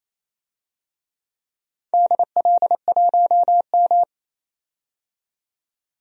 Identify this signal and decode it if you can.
Morse code, heard as a keyed tone, about 28 wpm, 700 Hz. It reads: DL1M
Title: DXCC-100 Call Signs - 4 Characters - 28wpm